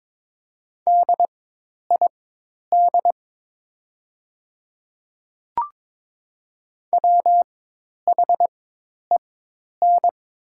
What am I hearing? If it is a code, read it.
Morse code, heard as a keyed tone, about 22 wpm, spaced as for 10 wpm, 700 Hz. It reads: DID WHEN
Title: 4x Character Spacing - Top 500 Words - 22wpm